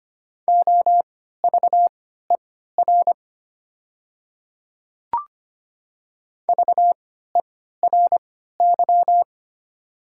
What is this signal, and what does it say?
Morse code, heard as a keyed tone, about 25 wpm, spaced as for 14 wpm, 700 Hz. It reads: OVER VERY
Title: 3x Character Spacing - Top 200 Words - 25wpm